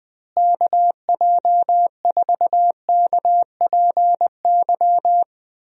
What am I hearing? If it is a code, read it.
Morse code, heard as a keyed tone, about 20 wpm, 700 Hz. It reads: KJ4KPY
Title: USA Call Signs - 6 Characters - Repeated 2x - 20wpm